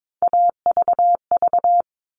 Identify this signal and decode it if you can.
Morse code, heard as a keyed tone, about 22 wpm, 700 Hz. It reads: AVV
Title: All Cut Numbers - 3-Digit 22wpm